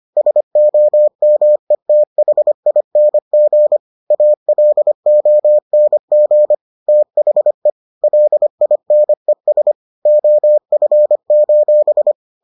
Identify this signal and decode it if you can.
Morse code, heard as a keyed tone, about 25 wpm, 600 Hz. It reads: SOMETHING ALONG THE LINES OF: